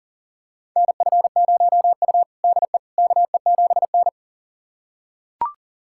Morse code, heard as a keyed tone, about 40 wpm, 700 Hz. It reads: NF0U DE KE7N
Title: USA Callsign DE Callsign - 4 Characters - 40wpm